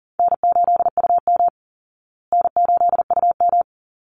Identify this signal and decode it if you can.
Morse code, heard as a keyed tone, about 40 wpm, 700 Hz. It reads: N8UM N8UM
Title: Easy Call Signs Repeated 2x - 40wpm